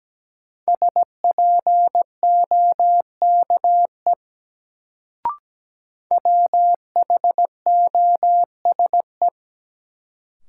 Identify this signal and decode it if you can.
Morse code, heard as a keyed tone, about 17 wpm, 700 Hz. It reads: SPOKE WHOSE
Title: Sentences from Top 1200 Words - New Word Drill - 17wpm